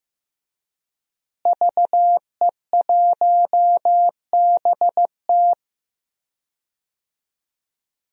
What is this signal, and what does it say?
Morse code, heard as a keyed tone, about 15 wpm, 700 Hz. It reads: VE1BT
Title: DXCC-100 Call Signs - 5 Characters - Repeated 2x - 15wpm